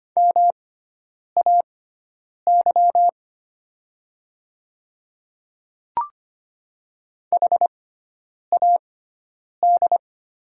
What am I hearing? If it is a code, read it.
Morse code, heard as a keyed tone, about 25 wpm, spaced as for 9 wpm, 700 Hz. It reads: MAY HAD
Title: 6x Character Spacing - Top 300 Words - 25wpm